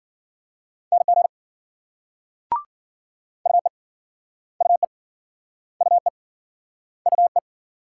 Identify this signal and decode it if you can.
Morse code, heard as a keyed tone, about 55 wpm, 700 Hz. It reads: NG UE UE UE UE